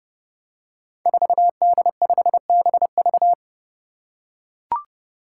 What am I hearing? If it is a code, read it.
Morse code, heard as a keyed tone, about 28 wpm, 700 Hz. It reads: 4D5BV
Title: Five Challenging Letter-Number Combinations 30wpm